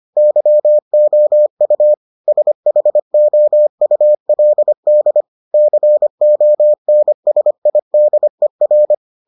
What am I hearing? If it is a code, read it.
Morse code, heard as a keyed tone, about 25 wpm, 600 Hz. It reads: YOU SHOULD CONSIDER